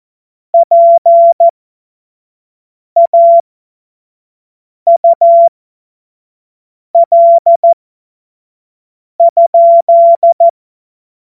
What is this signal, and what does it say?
Morse code, heard as a keyed tone, about 14 wpm, spaced as for 5 wpm, 680 Hz. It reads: PAUL?
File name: Morsekurs / Witze / Witz 28. Dezember 2021, 25 F70